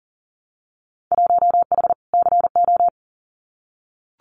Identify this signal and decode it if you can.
Morse code, heard as a keyed tone, about 40 wpm, 700 Hz. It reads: 1H CO
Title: Winter Field Day Exchanges 40wpm